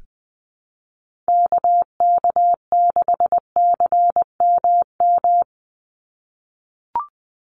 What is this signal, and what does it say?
Morse code, heard as a keyed tone, about 20 wpm, 700 Hz. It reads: KK6CMM
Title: USA Call Signs 20wpm